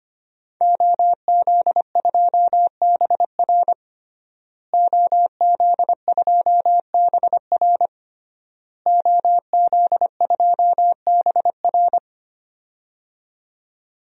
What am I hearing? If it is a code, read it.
Morse code, heard as a keyed tone, about 25 wpm, 700 Hz. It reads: OZ2BR OZ2BR OZ2BR
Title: DXCC-100 Call Signs - 5 Characters - Repeated 3x - 25wpm